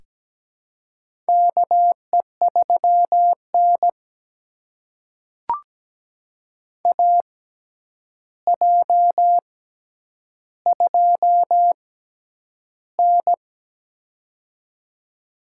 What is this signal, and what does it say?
Morse code, heard as a keyed tone, about 17 wpm, spaced as for 9 wpm, 700 Hz. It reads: KE3N AJ2N
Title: USA Call Signs - 4 Characters - 6x Character Spacing - 17wpm